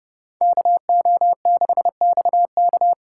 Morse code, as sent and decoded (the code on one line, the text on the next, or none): -.- --- -.... -..- -.-
KO6XK